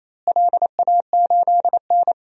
Morse code, heard as a keyed tone, about 27 wpm, 700 Hz. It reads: LA8N
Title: DXCC-100 Call Signs - 4 Characters - Repeated 3x - 28wpm